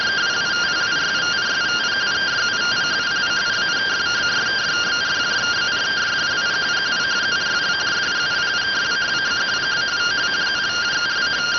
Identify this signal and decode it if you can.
Two-tone teleprinter signal, about 170 Hz shift.